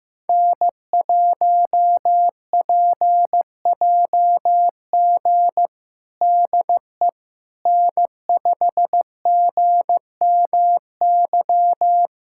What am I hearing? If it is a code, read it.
Morse code, heard as a keyed tone, about 15 wpm, 700 Hz. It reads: N1PJG DE N5GMY
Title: USA Callsign DE Callsign - 5 Characters - 15wpm